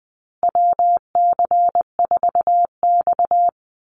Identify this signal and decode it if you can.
Morse code, heard as a keyed tone, about 19 wpm, 700 Hz. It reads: WC4X
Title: Call Signs Repeated 3x - 20wpm